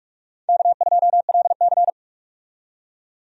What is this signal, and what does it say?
Morse code, heard as a keyed tone, about 46 wpm, 700 Hz. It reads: K2LC
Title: USA Call Signs - 4 Characters - 45wpm